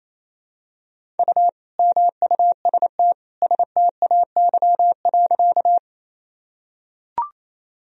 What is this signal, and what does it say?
Morse code, heard as a keyed tone, about 28 wpm, 700 Hz. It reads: U MUST STAY.
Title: Sentences from Top 500 Words and Abbreviations 28wpm